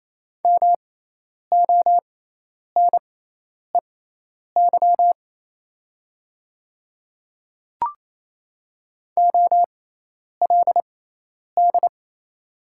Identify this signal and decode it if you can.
Morse code, heard as a keyed tone, about 28 wpm, spaced as for 10 wpm, 700 Hz. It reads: MONEY OLD
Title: 6x Character Spacing - Top 500 Words - 28wpm